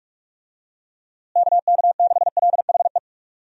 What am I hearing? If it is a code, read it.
Morse code, heard as a keyed tone, about 46 wpm, 700 Hz. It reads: KK6LHE
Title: USA Call Signs - 6 Characters - Repeated 2x - 45wpm